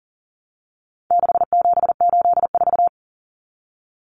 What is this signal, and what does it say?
Morse code, heard as a keyed tone, about 40 wpm, 700 Hz. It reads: 6784
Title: Common Cut Numbers - 4-Digit 40wpm